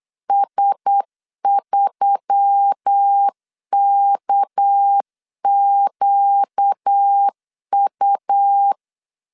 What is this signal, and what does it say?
Morse code, heard as a keyed tone, about 8 wpm, 800 Hz. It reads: S3KQU
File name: LEZIONI / 6 PKX0 / 6 PKX 8wpm 3041